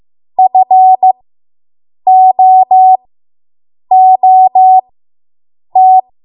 Morse code, heard as a keyed tone, about 15 wpm, spaced as for 7 wpm, 750 Hz. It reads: FOOT